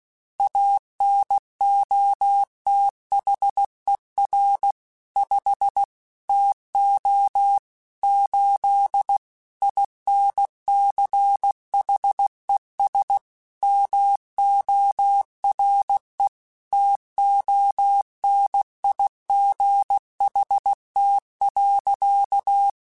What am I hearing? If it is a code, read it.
Morse code, heard as a keyed tone, about 16 wpm, 800 Hz. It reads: ANOTHER 5 TO 8 INCHES MORE TONIGHT.